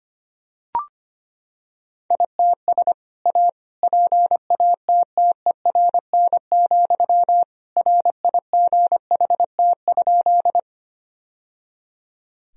Morse code, heard as a keyed tone, about 25 wpm, 700 Hz. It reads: ITS A PATTERN, RIGHT?
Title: Sentences from Top 1000 Words - Review Part 2 - 25wpm